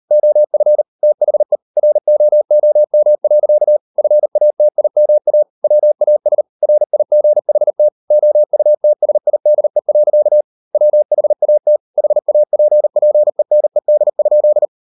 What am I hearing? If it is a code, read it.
Morse code, heard as a keyed tone, about 38 wpm, 600 Hz. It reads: OF THE ROOM. FATIMA WAS RIGHT OUTSIDE. WHAT HAPPENED?